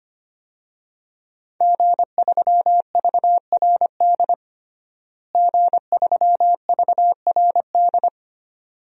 Morse code, heard as a keyed tone, about 25 wpm, 700 Hz. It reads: G3VRD G3VRD
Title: DXCC-100 Call Signs - 5 Characters - Repeated 2x - 25wpm